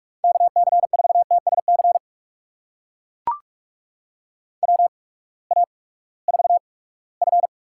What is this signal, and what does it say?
Morse code, heard as a keyed tone, about 42 wpm, spaced as for 22 wpm, 700 Hz. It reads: KC4TSC WA4F
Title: USA Call Signs - 6 Characters - 8x Character Spacing - 45wpm